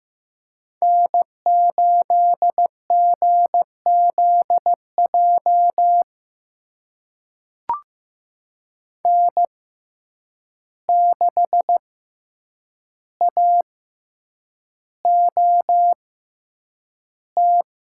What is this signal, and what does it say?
Morse code, heard as a keyed tone, about 15 wpm, spaced as for 8 wpm, 700 Hz. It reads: N8GZJ N6AOT
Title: USA Call Signs - 5 Characters - 6x Character Spacing - 15wpm